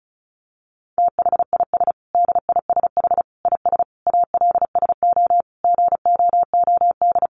Morse code, heard as a keyed tone, about 34 wpm, 700 Hz. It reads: THIS DISH IS ALSO GOOD
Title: Sentences from Top 1300 Words 35wpm